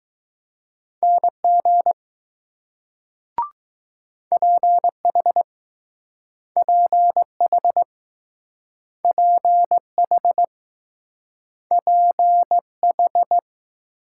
Morse code, heard as a keyed tone, about 21 wpm, 700 Hz. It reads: NG PH PH PH PH